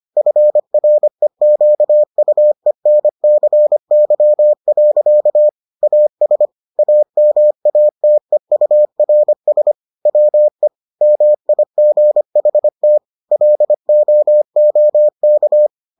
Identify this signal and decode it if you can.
Morse code, heard as a keyed tone, about 25 wpm, 600 Hz. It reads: FREQUENCY. AS AMATEURS WE MIGHT LOOK